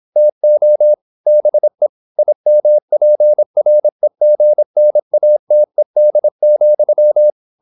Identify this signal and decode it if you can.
Morse code, heard as a keyed tone, about 26 wpm, 600 Hz. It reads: TO BE IMPREGNATED,
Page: Never 26wpm | CW med Gnister